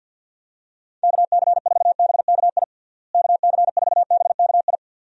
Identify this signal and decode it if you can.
Morse code, heard as a keyed tone, about 48 wpm, 700 Hz. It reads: KC4BKI KC4BKI
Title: USA Call Signs - 6 Characters - Repeated 3x - 50wpm